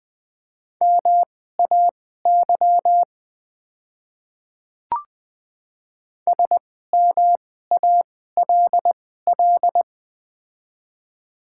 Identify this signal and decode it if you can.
Morse code, heard as a keyed tone, about 20 wpm, spaced as for 14 wpm, 700 Hz. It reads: MAY SMALL
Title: 2x Character Spacing - Top 300 Words - 20wpm